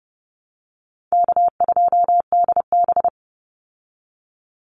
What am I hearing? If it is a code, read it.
Morse code, heard as a keyed tone, about 30 wpm, 700 Hz. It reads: K2DB
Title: Easy Call Signs Repeated 3x - 30wpm